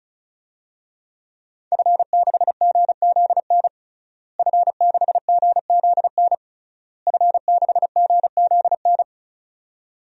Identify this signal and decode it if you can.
Morse code, heard as a keyed tone, about 36 wpm, 700 Hz. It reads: F6GZN F6GZN F6GZN
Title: DXCC-100 Call Signs - 5 Characters - Repeated 3x - 35wpm